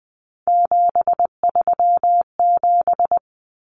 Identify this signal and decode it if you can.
Morse code, heard as a keyed tone, about 20 wpm, 700 Hz. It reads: 737
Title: Numbers - 3-Digit 20wpm